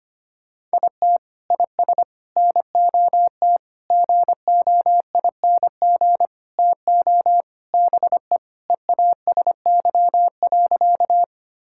Morse code, heard as a keyed tone, about 25 wpm, 700 Hz. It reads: IT IS NOT GOING TO BE EASY.
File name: practice / 25wpm / Sentences with 4-Word Phrases - 25wpm